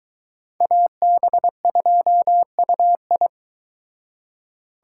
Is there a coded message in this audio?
Yes: Morse code from 0.6 to 3.3 s, heard as a keyed tone, about 22 wpm, 700 Hz. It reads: AB2UI